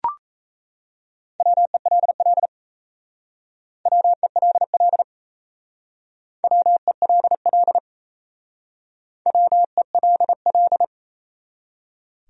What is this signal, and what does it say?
Morse code, heard as a keyed tone, about 36 wpm, 700 Hz. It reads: WELL WELL WELL WELL